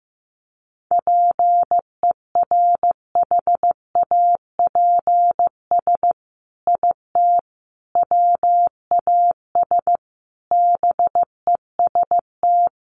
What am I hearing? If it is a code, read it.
Morse code, heard as a keyed tone, about 15 wpm, 700 Hz. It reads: PERHAPS IT WAS BEST
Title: Sentences from Top 600 Words 15wpm